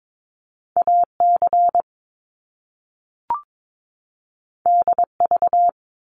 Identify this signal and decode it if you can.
Morse code, heard as a keyed tone, about 22 wpm, 700 Hz. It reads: AC DV
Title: Top Two Letter Combinations in Words 22wpm